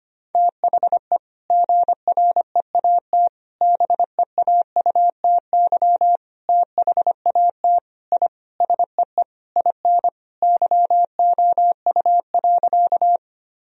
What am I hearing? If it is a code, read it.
Morse code, heard as a keyed tone, about 25 wpm, 700 Hz. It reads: THE GREAT BEAUTY THAT I SEE IN YOU.
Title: Sentences from Top 1000 Words - Review Part 2 - 25wpm